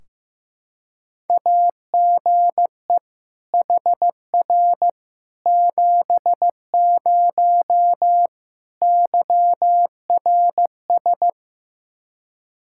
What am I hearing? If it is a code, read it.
Morse code, heard as a keyed tone, about 16 wpm, 700 Hz. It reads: AGE HR 70 YRS
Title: Practice QSO - Exchange 3 - More Info - 15wpm